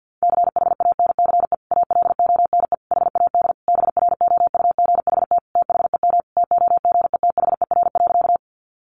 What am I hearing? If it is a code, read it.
Morse code, heard as a keyed tone, about 50 wpm, 700 Hz. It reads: CHANCE ALONE HAD BROUGHT THEM TOGETHER.
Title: Sentences from Top 1000 Words 50wpm